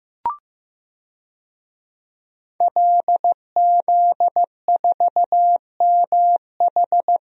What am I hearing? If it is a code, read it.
Morse code, heard as a keyed tone, about 15 wpm, 700 Hz. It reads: LZ4MH
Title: DXCC-100 Call Signs - 5 Characters - Repeated 3x - 15wpm